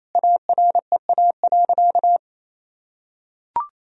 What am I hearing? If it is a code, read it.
Morse code, heard as a keyed tone, about 28 wpm, 700 Hz. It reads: AREA.
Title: Sentences from Top 1000 Words - Review Part 2 - 28wpm